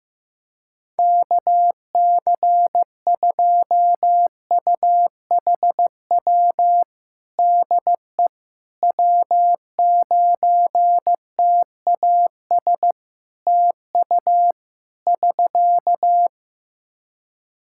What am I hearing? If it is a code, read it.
Morse code, heard as a keyed tone, about 15 wpm, 700 Hz. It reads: KC2UHW DE W9TAS TU <SK>
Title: Practice QSO - Exchange 4 - Ending - 15wpm